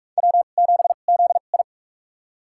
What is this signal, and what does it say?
Morse code, heard as a keyed tone, about 42 wpm, spaced as for 31 wpm, 690 Hz. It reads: W7ZI
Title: USA Call Signs - 4 Characters - 2x Character Spacing - 45wpm